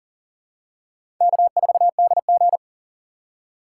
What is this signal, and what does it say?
Morse code, heard as a keyed tone, about 40 wpm, 700 Hz. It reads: K4DG
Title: USA Call Signs - 4 Characters - 4x Character Spacing - 40wpm